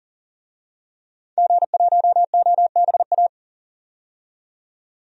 Morse code, heard as a keyed tone, about 40 wpm, 700 Hz. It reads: G1OBA
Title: DXCC-100 Call Signs - 5 Characters - 40wpm